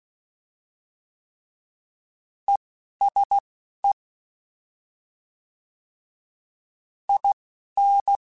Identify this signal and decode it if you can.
Morse code, heard as a keyed tone, about 16 wpm, 800 Hz. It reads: E S E I N